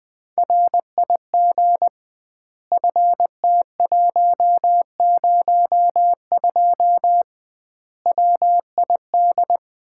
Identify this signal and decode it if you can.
Morse code, heard as a keyed tone, about 20 wpm, 700 Hz. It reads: RIG FT102 WID
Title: Common QSO Phrases - 2x Word-Spacing - 20wpm